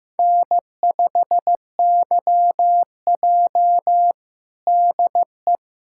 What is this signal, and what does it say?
Morse code, heard as a keyed tone, about 15 wpm, 700 Hz. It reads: N5YJ DE
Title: USA Callsign DE Callsign - 4 Characters - 15wpm